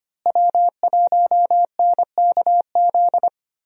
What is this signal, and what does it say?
Morse code, heard as a keyed tone, about 25 wpm, 700 Hz. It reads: W1NKZ
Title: USA Call Signs - 5 Characters - Repeated 3x - 25wpm